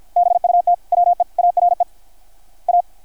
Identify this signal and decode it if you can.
Morse code, heard as a keyed tone, about 53 wpm, 660 Hz. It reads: BUT WE ARE A
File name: archived / CRSnet